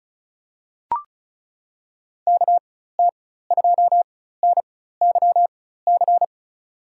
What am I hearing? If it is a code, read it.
Morse code, heard as a keyed tone, about 36 wpm, spaced as for 16 wpm, 700 Hz. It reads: KT2NYC
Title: USA Call Signs - 6 Characters - 4x Character Spacing - 35wpm